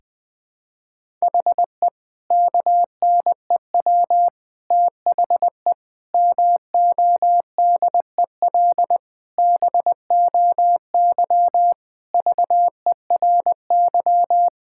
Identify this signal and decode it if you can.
Morse code, heard as a keyed tone, about 20 wpm, 700 Hz. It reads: HE KNEW THE MODEL BOY VERY
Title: Sentences from Top 1000 Words - Review Part 5 - 20wpm